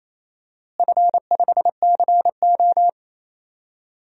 Morse code, heard as a keyed tone, about 28 wpm, 700 Hz. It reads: F5CO
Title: DXCC-100 Call Signs - 4 Characters - Repeated 2x - 28wpm